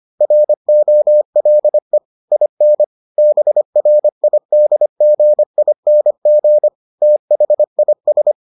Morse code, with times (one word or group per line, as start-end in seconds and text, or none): 0.2-2.0 ROLE
2.3-2.8 IN
3.2-6.7 BRIDGING
7.0-8.3 THIS